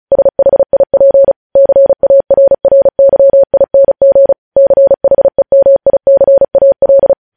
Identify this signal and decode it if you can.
Morse code, heard as a keyed tone, about 36 wpm, 550 Hz. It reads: SHIP CARRYING CHEMICAL
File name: podcast / News headlines-1x-35x35